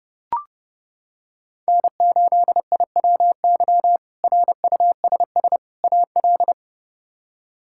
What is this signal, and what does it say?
Morse code, heard as a keyed tone, about 30 wpm, 700 Hz. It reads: N8IWY RUSS AL
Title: K1USN SST QSO Practice - 30wpm